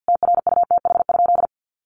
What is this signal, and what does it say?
Morse code, heard as a keyed tone, about 50 wpm, 700 Hz. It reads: TRUTH?
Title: Sentences from Top 1400 Words 50wpm